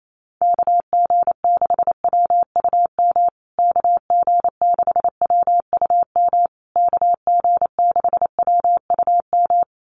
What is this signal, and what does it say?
Morse code, heard as a keyed tone, about 28 wpm, 700 Hz. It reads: KG6WUM KG6WUM KG6WUM
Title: USA Call Signs Repeated 3x 28wpm